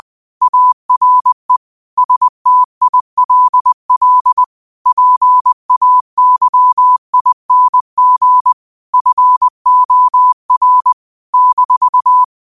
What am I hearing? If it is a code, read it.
Morse code, heard as a keyed tone, about 20 wpm, 1000 Hz. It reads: ARE STILL PAYING FOR -
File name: 20wpm